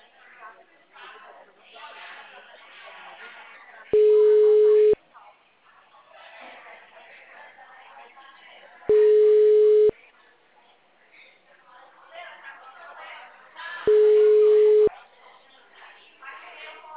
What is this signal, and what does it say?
Ringback tone (Europe, 425 Hz), with speech.
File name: out- -2021